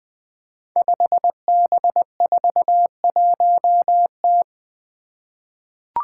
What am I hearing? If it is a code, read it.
Morse code, heard as a keyed tone, about 20 wpm, 700 Hz. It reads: 5B41T
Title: Five Challenging Letter-Number Combinations 20wpm